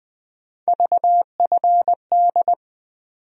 Morse code, heard as a keyed tone, about 20 wpm, 700 Hz. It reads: VFD
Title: USA Call Sign Postfixes - 3 Letters 20wpm